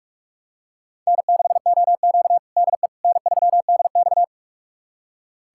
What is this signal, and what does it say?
Morse code, heard as a keyed tone, about 42 wpm, 700 Hz. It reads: N6OQ DE N3DX
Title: USA Callsign DE Callsign - 4 Characters - 45wpm